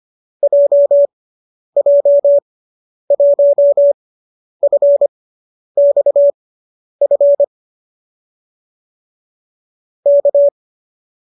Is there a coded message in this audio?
Yes: Morse code from 0.4 to 10.5 s, heard as a keyed tone, about 25 wpm, spaced as for 10 wpm, 570 Hz. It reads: JJ1FXF K